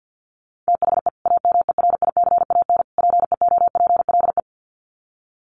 Morse code, heard as a keyed tone, about 50 wpm, 700 Hz. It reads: THE AMERICAN PEOPLE
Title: Sets of 3 Words 50wpm